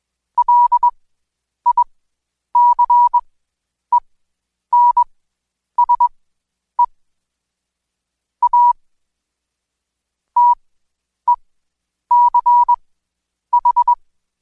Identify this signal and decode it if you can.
Morse code, heard as a keyed tone, about 20 wpm, spaced as for 9 wpm, 970 Hz. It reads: LICENSE A TECH
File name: Code Quick / AudioDownld2 / Seg115-Day20-DayDrill 20